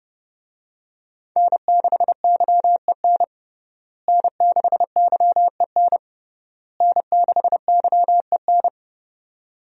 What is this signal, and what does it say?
Morse code, heard as a keyed tone, about 31 wpm, 700 Hz. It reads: N6YEN N6YEN N6YEN
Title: USA Call Signs - 5 Characters - Repeated 3x - 30wpm